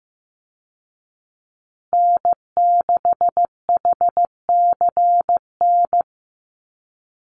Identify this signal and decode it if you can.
Morse code, heard as a keyed tone, about 15 wpm, 700 Hz. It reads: N6HCN
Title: Call Signs Repeated 2x - 15wpm